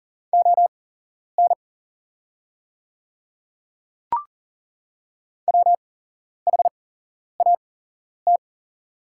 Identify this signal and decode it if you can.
Morse code, heard as a keyed tone, about 40 wpm, spaced as for 11 wpm, 690 Hz. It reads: ON WHAT
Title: 8x Character Spacing - Top 100 Words - 40wpm